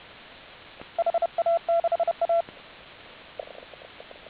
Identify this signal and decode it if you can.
Morse code, heard as a keyed tone, about 31 wpm, 1350 Hz, with noise. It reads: HA6A